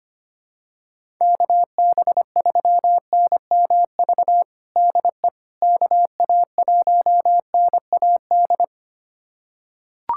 Morse code, heard as a keyed tone, about 25 wpm, 700 Hz. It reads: KB3NMV DE KA1NAD
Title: USA Callsign DE Callsign - 6 Characters - 25wpm